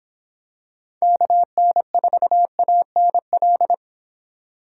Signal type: Morse code

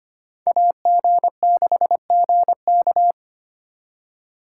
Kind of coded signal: Morse code

